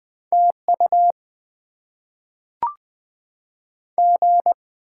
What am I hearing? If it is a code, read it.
Morse code, heard as a keyed tone, about 20 wpm, 700 Hz. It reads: TU G
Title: Top 20 QSO Elements - 6x Character Spacing - 20wpm